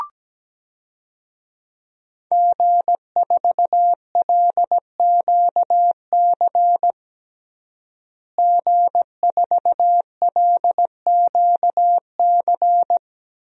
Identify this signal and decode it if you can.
Morse code, heard as a keyed tone, about 17 wpm, 700 Hz. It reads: G4LQC G4LQC
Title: DXCC-100 Call Signs - 5 Characters - Repeated 2x - 17wpm